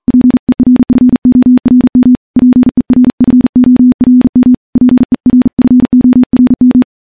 Morse code, heard as a keyed tone, about 40 wpm, 250 Hz. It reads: PERFORM PERFORM PERFORM
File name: morse-training-40WPM